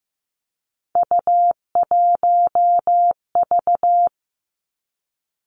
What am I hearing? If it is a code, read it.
Morse code, heard as a keyed tone, about 15 wpm, 700 Hz. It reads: U1V
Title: Three Challenging Letter-Number Combinations 15wpm